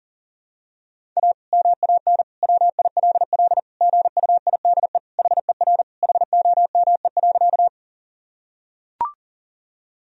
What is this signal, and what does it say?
Morse code, heard as a keyed tone, about 40 wpm, 700 Hz. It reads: A MAN WILL GUIDE HER HOME.
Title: Sentences from Top 1000 Words - Review Part 3 - 40wpm